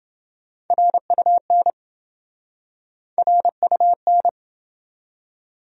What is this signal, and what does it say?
Morse code, heard as a keyed tone, about 30 wpm, 700 Hz. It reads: RUN RUN